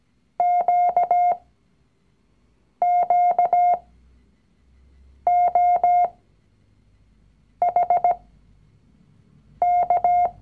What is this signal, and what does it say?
Morse code, heard as a keyed tone, about 17 wpm, spaced as for 5 wpm, 710 Hz. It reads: QQOHK
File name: CW-Online-Kurs / Anfänger / Lektion 019